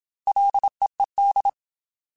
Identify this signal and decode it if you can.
Morse code, heard as a keyed tone, about 27 wpm, 800 Hz. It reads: LEED
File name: F and L / F and L mp3 / F and L Prefixes 3x2 Twice @25